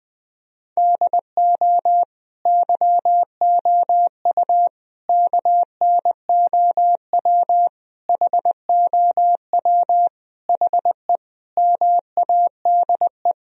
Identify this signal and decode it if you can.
Morse code, heard as a keyed tone, about 20 wpm, 700 Hz. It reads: DO YOU KNOW HOW HE MADE
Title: Sentences from Top 1000 Words - Review Part 1 - 20wpm